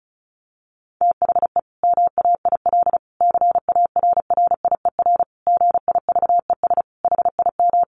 Morse code, heard as a keyed tone, about 34 wpm, 700 Hz. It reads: THE MAIL CARRIER GIVES HIM
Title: Additional Easy Phrases 35wpm